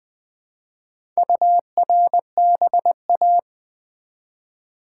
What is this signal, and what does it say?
Morse code, heard as a keyed tone, about 20 wpm, 700 Hz. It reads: URBA
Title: Build A Word - 1 Letter At A Time - 5-Letter Words - 20wpm